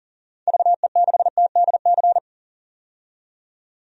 Morse code, heard as a keyed tone, about 37 wpm, 700 Hz. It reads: VE6TDC
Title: DXCC-100 Call Signs - 6 Characters - 40wpm